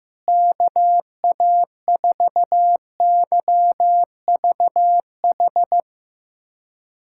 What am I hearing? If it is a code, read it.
Morse code, heard as a keyed tone, about 15 wpm, 700 Hz. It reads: KA4YVH
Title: USA Callsign DE Callsign 15wpm